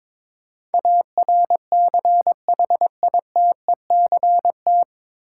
Morse code, heard as a keyed tone, about 22 wpm, 700 Hz. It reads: ARCHITECT
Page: Morse Code Ninja practice 22wpm